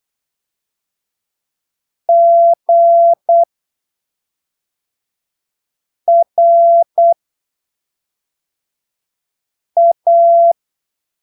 Morse code, heard as a keyed tone, about 8 wpm, spaced as for 3 wpm, 680 Hz. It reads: GRA
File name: arquivos / texto / texto 22 08ppm